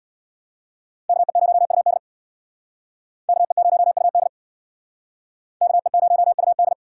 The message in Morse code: KE0UD KE0UD KE0UD

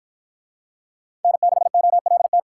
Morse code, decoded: N6OLT

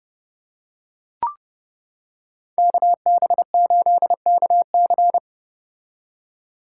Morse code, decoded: KB8KC

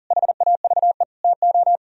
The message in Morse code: HAVE TO

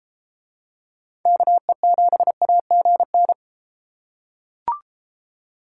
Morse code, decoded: KE7AGN

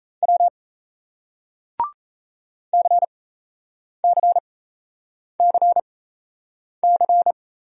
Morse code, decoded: W CCCC

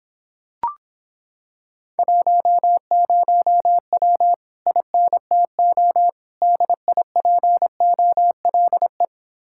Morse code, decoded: 10W INTO DIPOLE

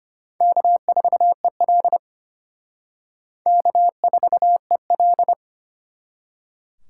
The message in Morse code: K4EL K4EL

